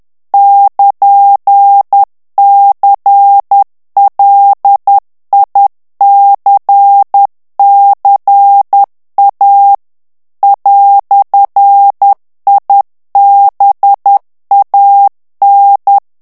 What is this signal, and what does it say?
Morse code, heard as a keyed tone, about 10 wpm, 800 Hz. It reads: <KN>CLICCA "IBAN
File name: morse mp3 / ITALIEN texte clair 1 vitesse 10 mots minute